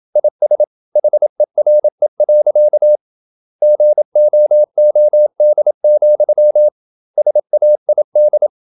Morse code, heard as a keyed tone, about 27 wpm, 600 Hz. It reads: IS HERE. GOOD, SAID
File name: Never 0027wpm